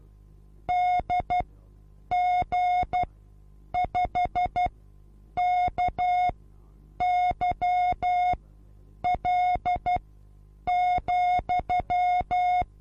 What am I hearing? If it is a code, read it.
Morse code, heard as a keyed tone, about 12 wpm, spaced as for 8 wpm, 720 Hz. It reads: DG5KYL,
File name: Onlinelehrgang / morsen / DJ4UF Morsen23